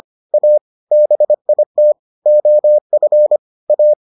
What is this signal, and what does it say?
Morse code, heard as a keyed tone, about 25 wpm, 600 Hz. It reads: A BIT OF A